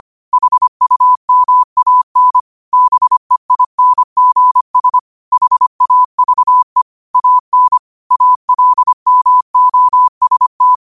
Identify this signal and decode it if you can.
Morse code, heard as a keyed tone, about 25 wpm, 1000 Hz. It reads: HUMAN BEINGS HAVE AN ALMOST
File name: qotdfiles / 25wpm